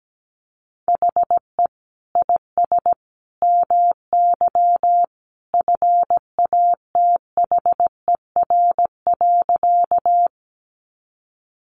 Morse code, heard as a keyed tone, about 17 wpm, 700 Hz. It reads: HE IS MY FATHER.
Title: Sentences from Top 300 Words 17wpm